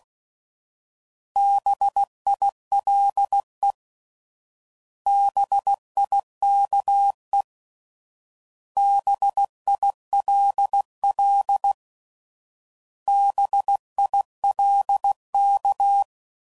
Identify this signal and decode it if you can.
Morse code, heard as a keyed tone, about 16 wpm, 800 Hz. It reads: BILE BIKE BILL BILK